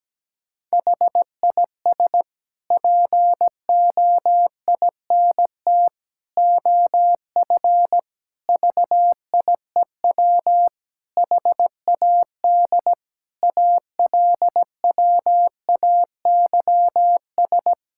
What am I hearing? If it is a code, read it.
Morse code, heard as a keyed tone, about 17 wpm, 700 Hz. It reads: HIS POINT OF VIEW HAD ALWAYS
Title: Sentences from Top 1000 Words - Review Part 6 - 17wpm